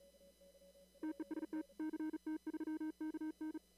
Morse code, heard as a keyed tone, about 37 wpm, 310 Hz. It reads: TEST CT3KN
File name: CW / kt5j wpx cw-op-k5pi